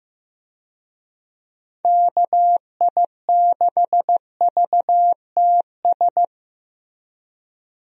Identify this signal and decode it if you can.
Morse code, heard as a keyed tone, about 15 wpm, 700 Hz. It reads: KI6VTS
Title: USA Call Signs - 6 Characters - Repeated 2x - 15wpm